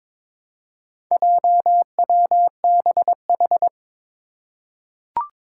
Morse code, heard as a keyed tone, about 22 wpm, 700 Hz. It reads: JWBH
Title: Four Challenging Letter-Number Combinations 22wpm